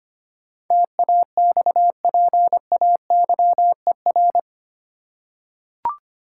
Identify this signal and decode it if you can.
Morse code, heard as a keyed tone, about 25 wpm, 700 Hz. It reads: TAXPAYER
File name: Eight Letter Words - 25wpm